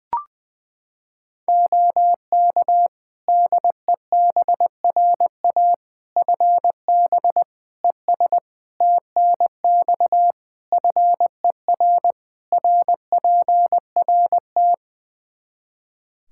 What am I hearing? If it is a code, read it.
Morse code, heard as a keyed tone, about 21 wpm, 700 Hz. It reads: OK DEBRA FB ES TNX FER RPRT
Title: Practice QSO - Exchange 2 - Info - 20wpm